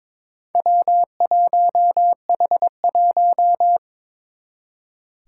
Morse code, heard as a keyed tone, about 22 wpm, 700 Hz. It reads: W1H1
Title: Four Challenging Letter-Number Combinations 22wpm